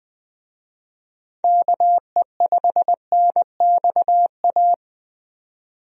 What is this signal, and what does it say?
Morse code, heard as a keyed tone, about 20 wpm, 700 Hz. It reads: KE5NXA
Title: USA Call Signs - 6 Characters - 20wpm